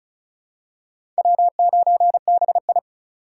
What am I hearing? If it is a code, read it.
Morse code, heard as a keyed tone, about 36 wpm, 700 Hz. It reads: W9BI
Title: USA Call Signs - 6x Character Spacing - 35wpm